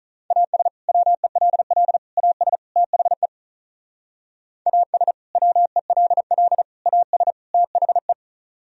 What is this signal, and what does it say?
Morse code, heard as a keyed tone, about 45 wpm, 700 Hz. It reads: AS WELL AS THE AS WELL AS THE